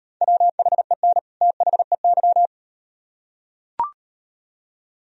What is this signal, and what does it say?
Morse code, heard as a keyed tone, about 38 wpm, 700 Hz. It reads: WHEN THEY